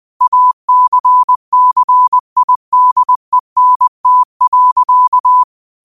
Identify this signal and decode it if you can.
Morse code, heard as a keyed tone, about 20 wpm, 1000 Hz. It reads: ACCIDENT.